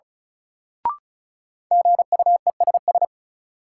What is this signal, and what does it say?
Morse code, heard as a keyed tone, about 36 wpm, 700 Hz. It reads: GUESS
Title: Sentences from Top 1500 Words - New Word Drill - 35wpm